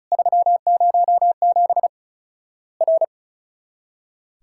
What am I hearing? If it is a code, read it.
Morse code, heard as a keyed tone, about 36 wpm, 700 Hz. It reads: 307 R
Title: Worked All Europe DX Contest - QTC Exchange - 35wpm